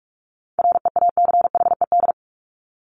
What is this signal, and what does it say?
Morse code, heard as a keyed tone, about 44 wpm, 700 Hz. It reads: REACHED
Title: ED Suffix - Top 100 Words with Base Words 45wpm